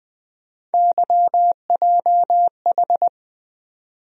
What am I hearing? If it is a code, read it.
Morse code, heard as a keyed tone, about 20 wpm, 700 Hz. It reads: YJH
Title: USA Call Sign Postfixes - 3 Letters 20wpm